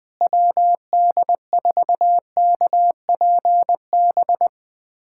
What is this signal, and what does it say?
Morse code, heard as a keyed tone, about 20 wpm, 700 Hz. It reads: WD4KPB
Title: USA Callsign DE Callsign 20wpm